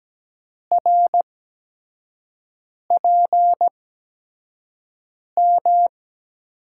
Morse code, heard as a keyed tone, about 17 wpm, spaced as for 5 wpm, 700 Hz. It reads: RPM
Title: USA Call Signs - 5 Characters - 8x Character Spacing - 17wpm